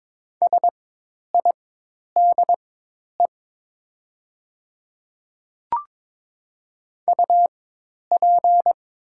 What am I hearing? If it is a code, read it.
Morse code, heard as a keyed tone, about 22 wpm, spaced as for 10 wpm, 700 Hz. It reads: SIDE UP
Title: 4x Character Spacing - Top 300 Words - 22wpm